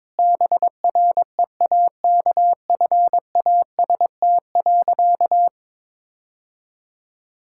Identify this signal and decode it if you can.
Morse code, heard as a keyed tone, about 22 wpm, 690 Hz. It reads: BREAKFAST.
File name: practice / 22wpm / Easy Sentences with Hints - 22wpm